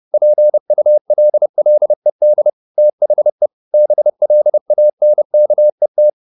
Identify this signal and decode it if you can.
Morse code, heard as a keyed tone, about 30 wpm, 600 Hz. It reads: PULLED THE BLANKET